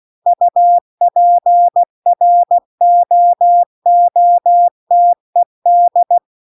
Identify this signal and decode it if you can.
Morse code, heard as a keyed tone, about 16 wpm, 700 Hz. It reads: UPROOTED
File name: CW-The-Big-Twoheaded-River-2-16WPM